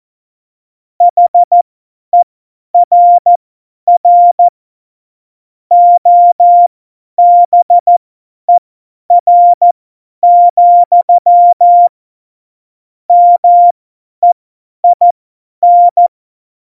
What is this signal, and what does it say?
Morse code, heard as a keyed tone, about 14 wpm, spaced as for 10 wpm, 700 Hz. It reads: HERR OBER, MEIN
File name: Morsekurs / Witze / Witz 14. Januar 2020, 50 F70